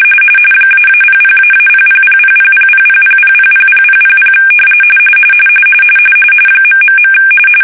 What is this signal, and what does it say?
RTTY, about 45.45 baud. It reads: --00--0000000000000000 ...
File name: fsk-ascii850-pix / fish